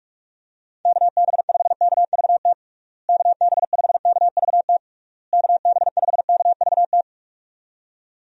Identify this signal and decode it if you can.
Morse code, heard as a keyed tone, about 46 wpm, 700 Hz. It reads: KB5KVT KB5KVT KB5KVT
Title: USA Call Signs - 6 Characters - Repeated 3x - 45wpm